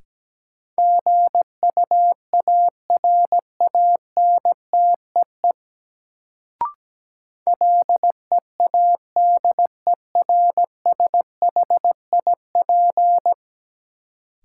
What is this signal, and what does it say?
Morse code, heard as a keyed tone, about 17 wpm, 700 Hz. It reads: GUARANTEE LEADERSHIP
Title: Sentences from Top 1600 Words - New Word Drill - 17wpm